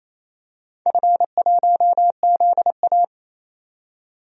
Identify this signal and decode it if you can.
Morse code, heard as a keyed tone, about 28 wpm, 700 Hz. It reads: F1ZA
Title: DXCC-100 Call Signs - 4 Characters - 28wpm